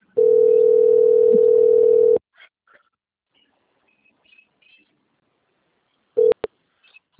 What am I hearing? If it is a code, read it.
Ringback tone (North America), 440 + 480 Hz.